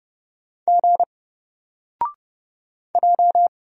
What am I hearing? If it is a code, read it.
Morse code, heard as a keyed tone, about 31 wpm, 700 Hz. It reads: G J
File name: Single-Letter-Number-30wpm